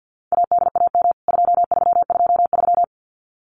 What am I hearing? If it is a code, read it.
Morse code, heard as a keyed tone, about 48 wpm, 700 Hz. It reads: ADAM 2323
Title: CWT Exchange 50wpm